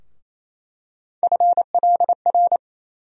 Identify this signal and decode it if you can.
Morse code, heard as a keyed tone, about 28 wpm, 700 Hz. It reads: FLR
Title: USA Call Sign Postfixes - 3 Letters 28wpm